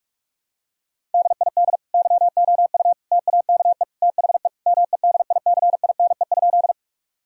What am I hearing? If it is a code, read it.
Morse code, heard as a keyed tone, about 46 wpm, 700 Hz. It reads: DID YOU TAKE THE MEDICINE?
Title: Sentences from Top 1000 Words - Review Part 6 - 45wpm